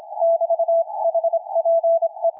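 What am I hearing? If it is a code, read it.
Morse code, heard as a keyed tone, about 27 wpm, 670 Hz, with radio noise. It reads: <BT> HPE
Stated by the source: Loud portable field-day CW on 40 m from a forest on a dipole, slightly drifting frequency, received through a 300 Hz filter, signal rising from 579 to 9.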